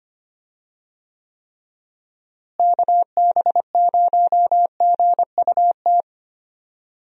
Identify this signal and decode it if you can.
Morse code, heard as a keyed tone, about 25 wpm, 700 Hz. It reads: KB0GUT
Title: USA Call Signs - 6 Characters - Repeated 3x - 25wpm